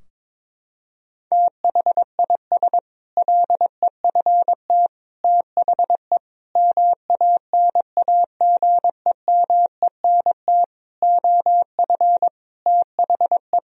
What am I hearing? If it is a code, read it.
Morse code, heard as a keyed tone, about 22 wpm, 700 Hz. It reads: THIS LEFT THE MANAGEMENT OF THE
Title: Sentences from Top 1000 Words - Review Part 4 - 22wpm